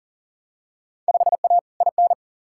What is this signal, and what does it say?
Morse code, heard as a keyed tone, about 40 wpm, 700 Hz. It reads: 5A IN
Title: Field Day Exchanges Encore - 40wpm